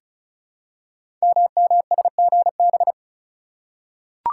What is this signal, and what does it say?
Morse code, heard as a keyed tone, about 36 wpm, 700 Hz. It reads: MMSGB
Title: Code Groups - 5 Letters - 35wpm